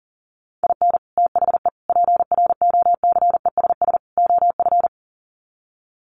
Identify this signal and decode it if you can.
Morse code, heard as a keyed tone, about 40 wpm, 700 Hz. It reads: IN THE PROCESS OF
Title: Sets of 4 Words 40wpm